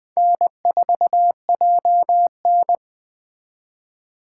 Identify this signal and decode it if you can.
Morse code, heard as a keyed tone, about 21 wpm, 700 Hz. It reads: N4JN
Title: Speed Racing - USA Call Signs 17wpm